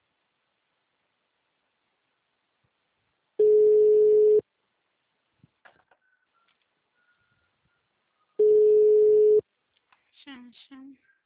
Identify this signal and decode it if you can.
Ringback tone (Europe, 425 Hz), with speech.